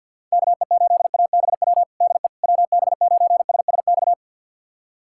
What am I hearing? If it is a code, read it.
Morse code, heard as a keyed tone, about 52 wpm, 700 Hz. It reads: KE8ABW DE WB9SSX
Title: USA Callsign DE Callsign - 6 Characters - 50wpm